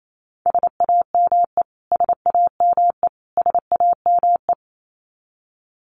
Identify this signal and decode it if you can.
Morse code, heard as a keyed tone, about 28 wpm, 700 Hz. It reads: SAME SAME SAME
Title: Top 200 Words Repeated 28wpm